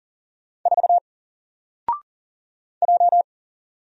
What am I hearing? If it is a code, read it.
Morse code, heard as a keyed tone, about 40 wpm, 700 Hz. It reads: 4 J
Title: Single Letter-Number 40wpm